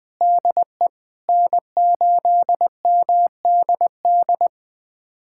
Morse code, heard as a keyed tone, about 20 wpm, 700 Hz. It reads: DE N8MDD
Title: USA Callsign DE Callsign - 5 Characters - 20wpm